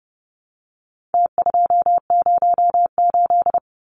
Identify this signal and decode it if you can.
Morse code, heard as a keyed tone, about 30 wpm, 700 Hz. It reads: T208
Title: Numbers - Contest Style - 4-Digit 30wpm